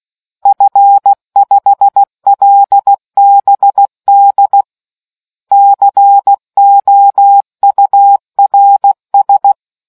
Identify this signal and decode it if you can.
Morse code, heard as a keyed tone, about 16 wpm, 800 Hz. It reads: F5LBD COURS